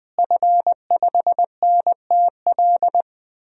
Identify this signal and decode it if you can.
Morse code, heard as a keyed tone, about 20 wpm, 700 Hz. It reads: F5NTL
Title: DXCC-100 Call Signs - 5 Characters - Repeated 2x - 20wpm